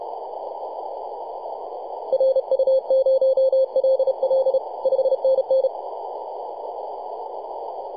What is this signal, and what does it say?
Morse code, heard as a keyed tone, about 31 wpm, 530 Hz, with radio noise. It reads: RU0LL 5NN
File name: posts / 2015-oct-03-2319z-tx3x-21-023mhz-cw